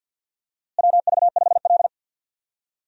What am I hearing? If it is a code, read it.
Morse code, heard as a keyed tone, about 52 wpm, 700 Hz. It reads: WV5L
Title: USA Call Signs - 4 Characters - Repeated 2x - 50wpm